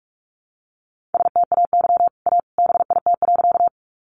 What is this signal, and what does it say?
Morse code, heard as a keyed tone, about 44 wpm, 700 Hz. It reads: STAY A BIT.
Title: Sentences from Top 900 Words 45wpm